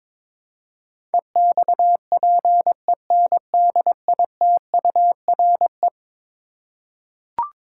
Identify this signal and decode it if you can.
Morse code, heard as a keyed tone, about 22 wpm, 700 Hz. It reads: EXPENDITURE
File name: practice / 22wpm / Eleven Letter Words - 22wpm